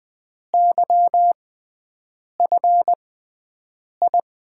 Morse code, heard as a keyed tone, about 20 wpm, spaced as for 7 wpm, 700 Hz. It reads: YFI
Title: USA Call Signs - 6 Characters - 6x Character Spacing - 20wpm